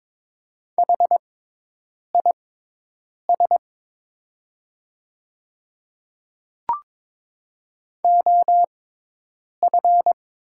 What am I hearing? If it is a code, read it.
Morse code, heard as a keyed tone, about 22 wpm, spaced as for 8 wpm, 700 Hz. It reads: HIS OF
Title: 6x Character Spacing - Top 100 Words - 22wpm